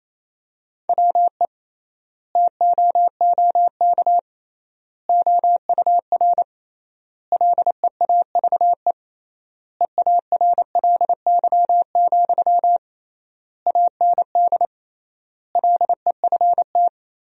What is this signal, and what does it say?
Morse code, heard as a keyed tone, about 28 wpm, 700 Hz. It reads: WE TOOK OUR LEAVE EARLY, AND LEFT
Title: 3x Word-Spacing - Sentences from Top 400 Words - 28wpm